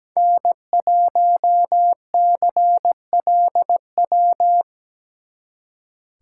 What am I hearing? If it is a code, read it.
Morse code, heard as a keyed tone, about 19 wpm, 700 Hz. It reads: N1CLW
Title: Speed Racing - USA Call Signs 17wpm